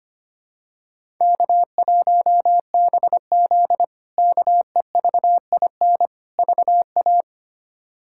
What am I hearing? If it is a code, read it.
Morse code, heard as a keyed tone, about 25 wpm, 700 Hz. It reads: K1BZ KEVIN VA
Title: K1USN SST QSO Practice - 25wpm